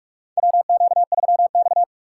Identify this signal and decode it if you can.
Morse code, heard as a keyed tone, about 46 wpm, 700 Hz. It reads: WQ3X
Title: USA Call Signs - 4 Characters - 4x Character Spacing - 45wpm